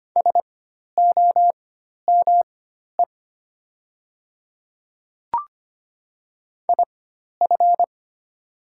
Morse code, heard as a keyed tone, about 25 wpm, spaced as for 12 wpm, 700 Hz. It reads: SOME IF
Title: 4x Character Spacing - Top 100 Words - 25wpm